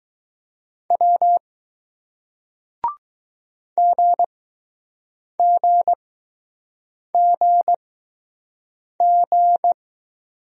Morse code, heard as a keyed tone, about 20 wpm, spaced as for 6 wpm, 700 Hz. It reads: W GGGG